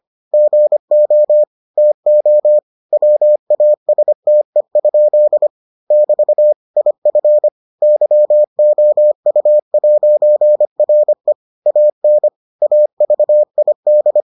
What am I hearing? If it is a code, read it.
Morse code, heard as a keyed tone, about 25 wpm, 600 Hz. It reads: GO TO WASTE? <BT> IF YOU'RE AN AVID